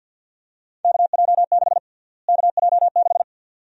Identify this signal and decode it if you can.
Morse code, heard as a keyed tone, about 51 wpm, 700 Hz. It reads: KJ6 KJ6
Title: USA Call Sign Prefixes - Repeated 2x - 50wpm